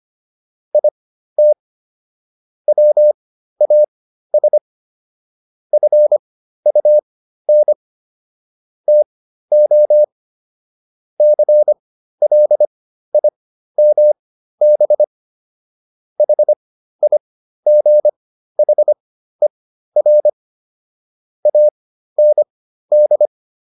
Morse code, heard as a keyed tone, about 25 wpm, spaced as for 13 wpm, 600 Hz. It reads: IT WAS FUN TO CLIMB HIGHER AND